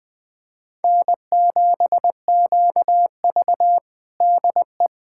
Morse code, heard as a keyed tone, about 20 wpm, 700 Hz. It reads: N7QV DE
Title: USA Callsign DE Callsign - 4 Characters - 20wpm